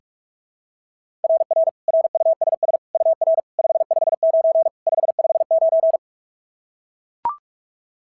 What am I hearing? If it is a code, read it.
Morse code, heard as a keyed tone, about 44 wpm, 690 Hz. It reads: RR RUSS UR 559 559
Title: SOTA QSO Practice 45wpm